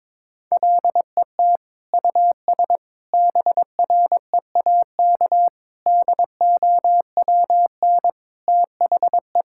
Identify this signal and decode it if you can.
Morse code, heard as a keyed tone, about 22 wpm, 700 Hz. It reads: LET US BREAK DOWN THE
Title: Sentences from Top 1000 Words - Review Part 6 - 22wpm